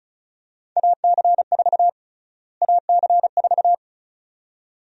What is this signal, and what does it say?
Morse code, heard as a keyed tone, about 36 wpm, 700 Hz. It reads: AC4 AC4
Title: USA Call Sign Prefixes - Repeated 2x - 35wpm